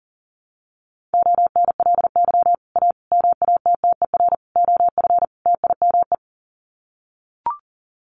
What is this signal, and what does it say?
Morse code, heard as a keyed tone, about 37 wpm, 700 Hz. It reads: ONLY A MATTER OF TIME
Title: Sets of 5 Words Encore 40wpm